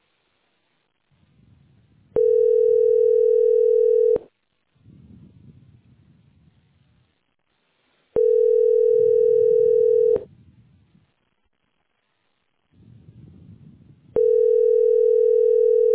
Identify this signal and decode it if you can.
Ringback tone (North America), 440 + 480 Hz.